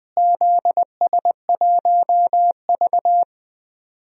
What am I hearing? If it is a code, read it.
Morse code, heard as a keyed tone, about 21 wpm, 700 Hz. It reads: ZS1V
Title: DXCC-100 Call Signs - 4 Characters - Repeated 3x - 20wpm